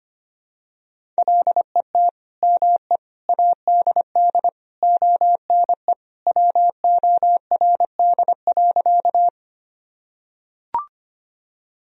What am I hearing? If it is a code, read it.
Morse code, heard as a keyed tone, about 25 wpm, 700 Hz. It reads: LET ME ADD ONE WORD.
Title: Sentences from Top 1000 Words - Review Part 1 - 25wpm